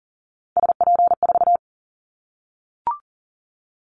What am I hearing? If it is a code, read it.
Morse code, heard as a keyed tone, about 40 wpm, 700 Hz. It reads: SP4
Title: Callsign Prefixes for Top 25 DXCC - 40wpm